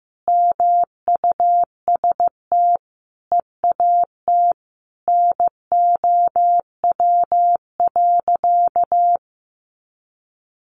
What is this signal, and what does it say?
Morse code, heard as a keyed tone, about 15 wpm, 700 Hz. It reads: MUST EAT NOW.
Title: Sentences from Top 300 Words 15wpm